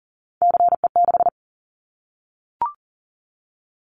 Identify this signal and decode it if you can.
Morse code, heard as a keyed tone, about 40 wpm, 700 Hz. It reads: CE6
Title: Callsign Prefixes for Top 100 DXCC - 40wpm